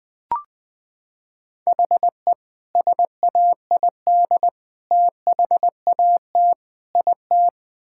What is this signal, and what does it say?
Morse code, heard as a keyed tone, about 21 wpm, 700 Hz. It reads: HE SAID THAT IT
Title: Sentences from Top 1000 Words - Review Part 6 - 20wpm